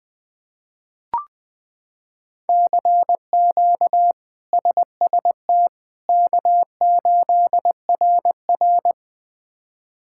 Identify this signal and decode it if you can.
Morse code, heard as a keyed tone, about 20 wpm, 700 Hz. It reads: CQ SST K8RR
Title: K1USN SST QSO Practice - 20wpm